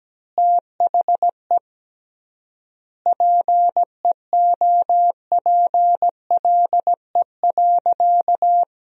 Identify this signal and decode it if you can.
Morse code, heard as a keyed tone, about 17 wpm, 700 Hz. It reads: THE PEOPLE.
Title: 3x Word-Spacing - Sentences from Top 400 Words - 17wpm